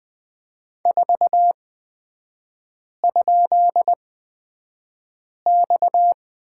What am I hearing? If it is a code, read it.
Morse code, heard as a keyed tone, about 20 wpm, spaced as for 5 wpm, 700 Hz. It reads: 4?X
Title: 243 Lesson 39e - Letter z - Three Characters 20wpm